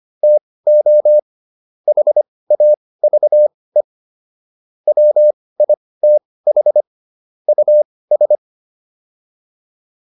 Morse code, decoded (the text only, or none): TO HAVE WITH US